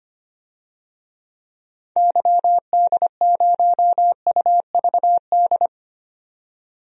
Morse code, heard as a keyed tone, about 25 wpm, 700 Hz. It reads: YD0UVD
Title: DXCC-100 Call Signs - 6 Characters - 25wpm